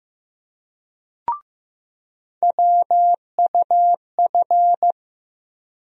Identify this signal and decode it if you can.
Morse code, heard as a keyed tone, about 15 wpm, 700 Hz. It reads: WUF
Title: USA Call Sign Postfixes - 3 Letters 15wpm